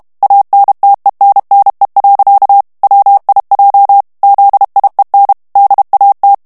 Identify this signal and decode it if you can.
Morse code, heard as a keyed tone, about 32 wpm, 800 Hz. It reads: ANTENNE. WIJ ZIEN DAT
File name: morse mp3 / HOLLANDAIS texte clair 1 vitesse 30 mots minute